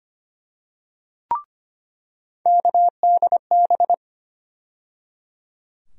Morse code, heard as a keyed tone, about 25 wpm, 700 Hz. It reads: KDB
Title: USA Call Sign Postfixes - 3 Letters 25wpm